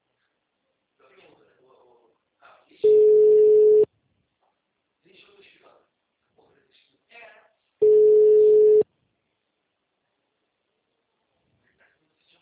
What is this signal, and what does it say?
Ringback tone (Europe, 425 Hz), with speech.